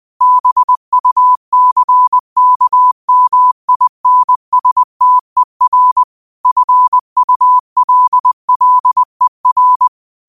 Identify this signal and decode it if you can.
Morse code, heard as a keyed tone, about 20 wpm, 1000 Hz. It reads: BUCKMINSTER FULLER
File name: qotdfiles / 20wpm